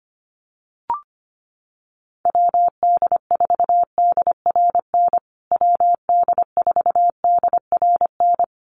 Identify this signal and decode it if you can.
Morse code, heard as a keyed tone, about 25 wpm, 700 Hz. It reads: WD4DRN WD4DRN
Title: USA Call Signs Repeated 2x 25wpm